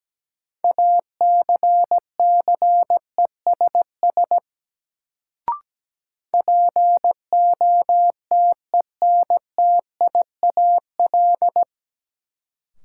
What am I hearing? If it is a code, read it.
Morse code, heard as a keyed tone, about 17 wpm, 700 Hz. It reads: ACCESS POTENTIAL
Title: Sentences from Top 1000 Words - Review Part 4 - New Word Drill - 17wpm